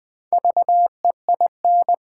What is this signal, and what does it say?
Morse code, heard as a keyed tone, about 20 wpm, 700 Hz. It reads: VEIN
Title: Build A Word - 2 Letters At A Time - 4-Letter Words - 20wpm